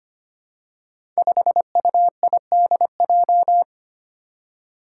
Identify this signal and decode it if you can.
Morse code, heard as a keyed tone, about 25 wpm, 700 Hz. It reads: 5UIDJ
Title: Five Challenging Letter-Number Combinations 25wpm